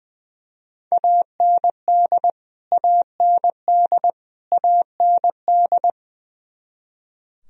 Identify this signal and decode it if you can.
Morse code, heard as a keyed tone, about 21 wpm, 700 Hz. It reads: AND AND AND
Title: Top 100 Words Repeated 20wpm